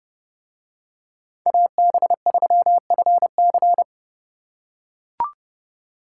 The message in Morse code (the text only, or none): AB3FC